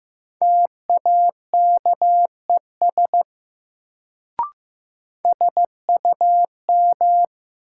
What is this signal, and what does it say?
Morse code, heard as a keyed tone, about 15 wpm, 700 Hz. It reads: TAKES SUM
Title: Sentences from Top 1000 Words - Review Part 6 - New Word Drill - 15wpm